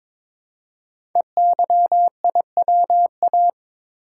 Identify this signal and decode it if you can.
Morse code, heard as a keyed tone, about 23 wpm, 700 Hz. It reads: EYIWA
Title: Code Groups - 5 Letters - 22wpm